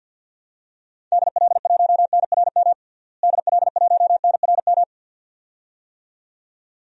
Morse code, heard as a keyed tone, about 51 wpm, 700 Hz. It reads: DL1NRM DL1NRM
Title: DXCC-100 Call Signs - 6 Characters - Repeated 2x - 50wpm